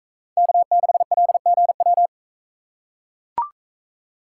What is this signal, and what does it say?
Morse code, heard as a keyed tone, about 40 wpm, 700 Hz. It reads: K6LGW